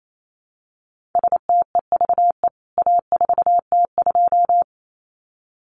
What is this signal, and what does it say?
Morse code, heard as a keyed tone, about 28 wpm, 700 Hz. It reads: STEVE A4T2
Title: CWT Exchange 28wpm